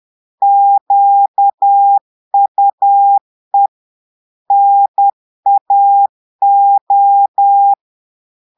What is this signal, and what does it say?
Morse code, heard as a keyed tone, about 10 wpm, 800 Hz. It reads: QUE NAO
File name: Portugais - texte en clair 1 - vitesse - 10 mots minute